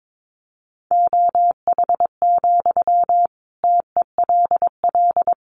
Morse code, heard as a keyed tone, about 22 wpm, 700 Hz. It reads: OH, TELL
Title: Sentences from Top 500 Words 22wpm